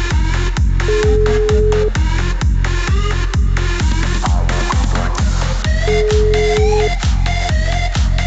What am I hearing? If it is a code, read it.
Ringback tone (Europe, 425 Hz), with noise.